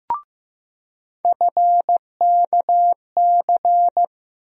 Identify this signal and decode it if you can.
Morse code, heard as a keyed tone, about 15 wpm, 700 Hz. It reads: FKC
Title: USA Call Sign Postfixes - 3 Letters 15wpm